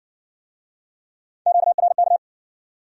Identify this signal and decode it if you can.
Morse code, heard as a keyed tone, about 57 wpm, 700 Hz. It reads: QRK